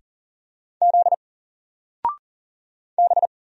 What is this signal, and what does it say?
Morse code, heard as a keyed tone, about 40 wpm, 700 Hz. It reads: Z B
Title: Single Letter-Number Alt 40wpm